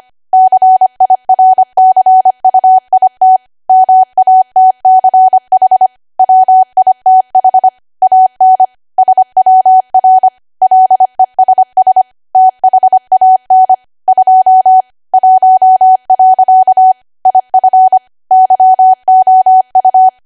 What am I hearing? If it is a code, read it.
Morse code, heard as a keyed tone, about 25 wpm, 750 Hz. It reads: CIRCUIT MATCH WITH AN SWR LESS THAN 2 1. IF YOU
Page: 25 WPM Code Archive